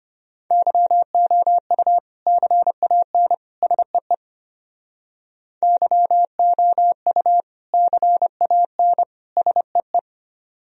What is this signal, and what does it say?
Morse code, heard as a keyed tone, about 32 wpm, 700 Hz. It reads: YOU CAN SEE YOU CAN SEE